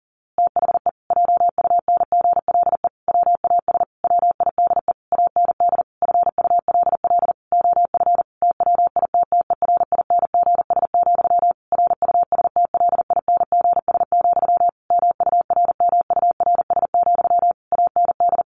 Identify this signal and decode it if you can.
Morse code, heard as a keyed tone, about 40 wpm, 700 Hz. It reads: THE JUNGLE WAS WIDE AND FULL OF TWITTERINGS, RUSTLINGS, MURMURS, AND
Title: A Sound Of Thunder 40wpm